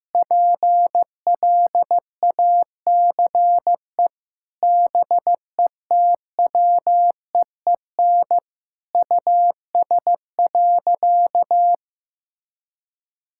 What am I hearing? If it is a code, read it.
Morse code, heard as a keyed tone, about 16 wpm, 700 Hz. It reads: PLACE BETWEEN US.
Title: Sentences from Top 1000 Words - Review Part 6 - 15wpm